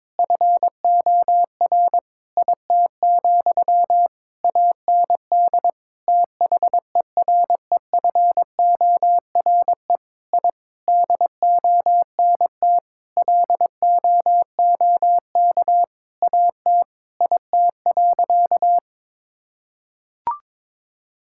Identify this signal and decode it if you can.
Morse code, heard as a keyed tone, about 22 wpm, 700 Hz. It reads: FOR IT, AND THEREFORE I DONT LOOK AT IT.
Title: Sentences from Top 1000 Words - Review Part 5 - 22wpm